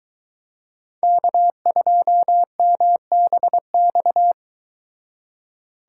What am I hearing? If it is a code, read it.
Morse code, heard as a keyed tone, about 23 wpm, 700 Hz. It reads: K2MBX